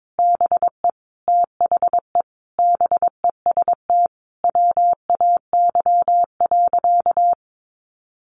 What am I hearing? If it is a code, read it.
Morse code, heard as a keyed tone, about 22 wpm, 700 Hz. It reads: BE THE BEST WAY.
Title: Sentences from Top 400 Words 22wpm